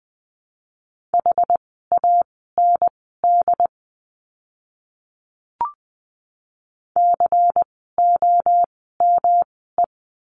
Morse code, heard as a keyed tone, about 19 wpm, spaced as for 14 wpm, 700 Hz. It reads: HAND COME
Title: 244 Lesson 39f - Letter z - Words 20wpm